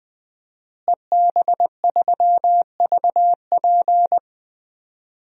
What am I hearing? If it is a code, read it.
Morse code, heard as a keyed tone, about 21 wpm, 700 Hz. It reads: EB3VP
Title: DXCC-100 Call Signs - 5 Characters - Repeated 3x - 20wpm